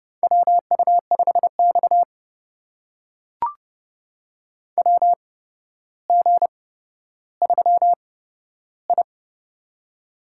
Morse code, heard as a keyed tone, about 31 wpm, spaced as for 13 wpm, 700 Hz. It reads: WU5X WG3I
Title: USA Call Signs - 4 Characters - 8x Character Spacing - 30wpm